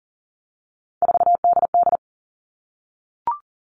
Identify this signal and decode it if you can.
Morse code, heard as a keyed tone, about 40 wpm, 700 Hz. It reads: 4DD
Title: All Cut Numbers - 3-Digit 40wpm